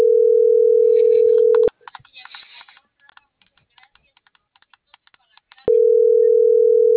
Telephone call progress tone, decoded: ringback tone

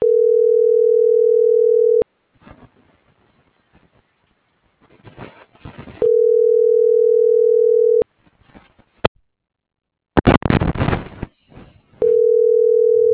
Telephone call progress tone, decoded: ringback tone

